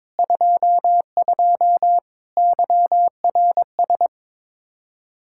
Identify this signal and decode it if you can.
Morse code, heard as a keyed tone, about 22 wpm, 700 Hz. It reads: 22 YRS
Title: Practice QSO - Exchange 3 - More Info - 22wpm